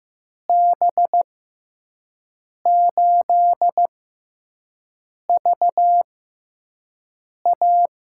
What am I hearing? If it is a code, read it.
Morse code, heard as a keyed tone, about 15 wpm, spaced as for 5 wpm, 700 Hz. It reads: B8VA
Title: USA Call Signs - 6 Characters - 6x Character Spacing - 15wpm